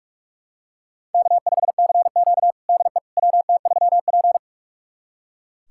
Morse code, heard as a keyed tone, about 41 wpm, 700 Hz. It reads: K5CQ DE WT3P
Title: USA Callsign DE Callsign - 4 Characters - 45wpm